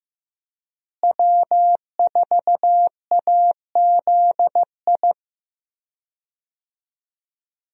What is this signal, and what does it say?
Morse code, heard as a keyed tone, about 15 wpm, 700 Hz. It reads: W4AZI
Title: USA Call Signs - 5 Characters - Repeated 3x - 15wpm